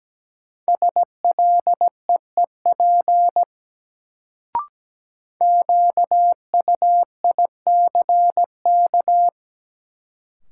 Morse code, heard as a keyed tone, about 17 wpm, 700 Hz. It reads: SLEEP QUICK
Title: Sentences from Top 500 Words - New Word Drill - 17wpm